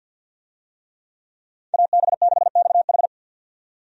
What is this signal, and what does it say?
Morse code, heard as a keyed tone, about 52 wpm, 700 Hz. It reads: AB6XH
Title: USA Call Signs - 5 Characters - 50wpm